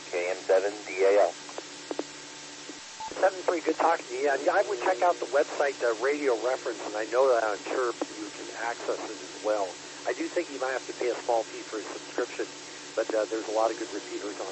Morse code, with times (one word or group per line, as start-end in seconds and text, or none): none